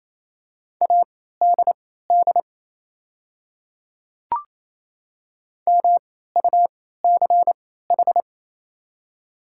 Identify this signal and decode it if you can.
Morse code, heard as a keyed tone, about 28 wpm, spaced as for 16 wpm, 700 Hz. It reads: ADD MUCH
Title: 3x Character Spacing - Top 300 Words - 28wpm